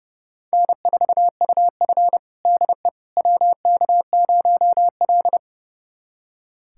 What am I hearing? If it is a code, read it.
Morse code, heard as a keyed tone, about 28 wpm, 700 Hz. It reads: N4UF DE WK0L
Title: USA Callsign DE Callsign - 4 Characters - 30wpm